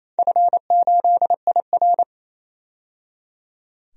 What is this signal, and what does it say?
Morse code, heard as a keyed tone, about 28 wpm, 700 Hz. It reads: F8IR
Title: DXCC-100 Call Signs - 4 Characters - 28wpm